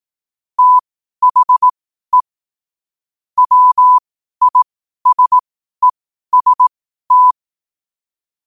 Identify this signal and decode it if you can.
Morse code, heard as a keyed tone, about 18 wpm, spaced as for 12 wpm, 1000 Hz. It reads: THE WISEST
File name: qotdfiles / 12wpm